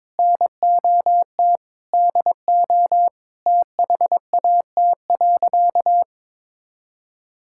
Morse code, heard as a keyed tone, about 22 wpm, 700 Hz. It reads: NOT DO THAT.
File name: practice / 22wpm / Sentences with 3-Word Phrases - 22wpm